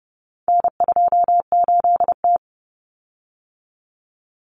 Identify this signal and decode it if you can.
Morse code, heard as a keyed tone, about 30 wpm, 700 Hz. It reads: N28T
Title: Common Cut Numbers - 4-Digit 30wpm